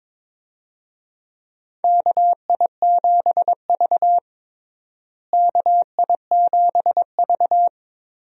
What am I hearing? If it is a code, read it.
Morse code, heard as a keyed tone, about 22 wpm, 700 Hz. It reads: KI7V KI7V
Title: USA Call Signs - 4 Characters - Repeated 2x - 22wpm